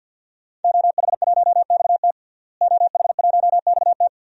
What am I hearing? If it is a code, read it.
Morse code, heard as a keyed tone, about 50 wpm, 700 Hz. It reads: OH1XT OH1XT
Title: DXCC-100 Call Signs - 5 Characters - Repeated 3x - 50wpm